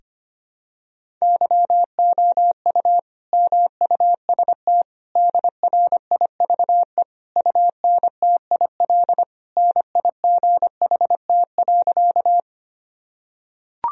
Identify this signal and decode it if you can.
Morse code, heard as a keyed tone, about 25 wpm, 700 Hz. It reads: YOU MUST DRIVE UNTIL NIGHT.
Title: Sentences from Top 1000 Words - Review Part 2 - 25wpm